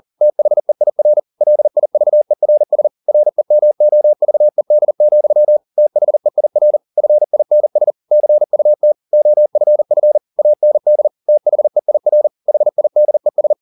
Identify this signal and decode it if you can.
Morse code, heard as a keyed tone, about 40 wpm, 600 Hz. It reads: THEIR LIVERS REMOVED, THEIR FINS CUT OFF AND THEIR HIDES